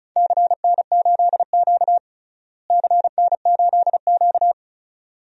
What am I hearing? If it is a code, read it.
Morse code, heard as a keyed tone, about 34 wpm, 700 Hz. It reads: CN8Q CN8Q
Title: DXCC-100 Call Signs - 4 Characters - Repeated 2x - 35wpm